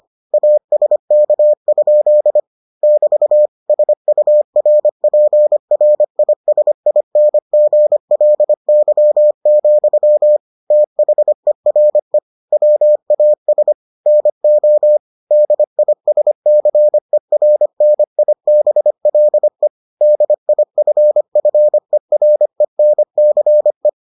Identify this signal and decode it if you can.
Morse code, heard as a keyed tone, about 25 wpm, 600 Hz. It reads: ASK? <BT> SURPRISINGLY, THERE WAS NO DISCERNIBLE DIFFERENCE